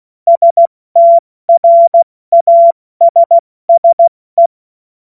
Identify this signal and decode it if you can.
Morse code, heard as a keyed tone, about 16 wpm, 680 Hz. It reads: STRASSE
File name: Morsekurs / Witze / Witz 03. Januar 2023 70 F80